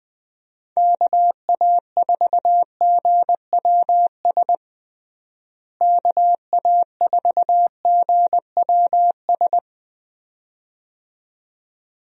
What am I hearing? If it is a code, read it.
Morse code, heard as a keyed tone, about 20 wpm, 700 Hz. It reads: KA4GWS KA4GWS
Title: USA Call Signs - 6 Characters - Repeated 2x - 20wpm